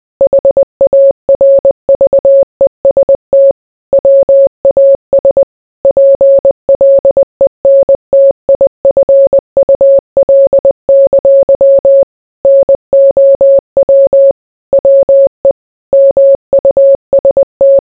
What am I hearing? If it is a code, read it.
Morse code, heard as a keyed tone, about 20 wpm, 550 Hz. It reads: HARVEST WAS PLENTIFUL! NOW WE MUST